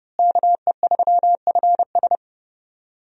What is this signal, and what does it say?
Morse code, heard as a keyed tone, about 30 wpm, 700 Hz. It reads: KE3FS
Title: USA Call Signs - 5 Characters - Repeated 2x - 30wpm